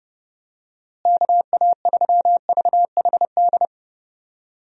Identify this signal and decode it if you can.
Morse code, heard as a keyed tone, about 30 wpm, 700 Hz. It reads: KA3VHD
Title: USA Call Signs - 6 Characters - Repeated 2x - 30wpm